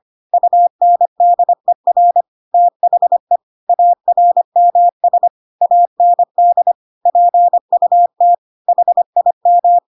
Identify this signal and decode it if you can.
Morse code, heard as a keyed tone, about 25 wpm, 700 Hz. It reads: UNDER THE ARMS AND PUT HIM
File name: Pinocchio / Pinocchio ch3 25 25 ik1hge